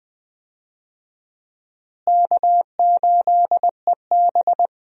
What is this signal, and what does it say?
Morse code, heard as a keyed tone, about 20 wpm, 700 Hz. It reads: K8EB
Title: USA Call Signs - 4 Characters - Repeated 2x - 20wpm